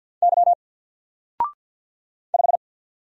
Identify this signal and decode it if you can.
Morse code, heard as a keyed tone, about 50 wpm, 700 Hz. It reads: Y 5
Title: Single Letter-Number Alt 50wpm